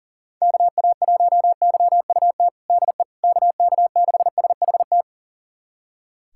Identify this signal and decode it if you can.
Morse code, heard as a keyed tone, about 40 wpm, 700 Hz. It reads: KA1YUT DE KK6SHT
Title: USA Callsign DE Callsign 40wpm